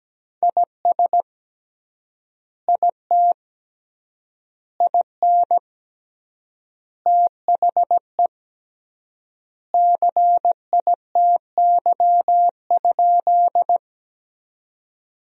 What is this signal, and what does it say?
Morse code, heard as a keyed tone, about 17 wpm, 700 Hz. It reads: IS IT IN THE CITY?
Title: 3x Word-Spacing - Sentences from Top 300 Words - 17wpm